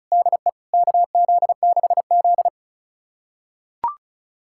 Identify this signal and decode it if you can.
Morse code, heard as a keyed tone, about 34 wpm, 700 Hz. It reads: DE KZ6Z
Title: USA Callsign DE Callsign - 4 Characters - 35wpm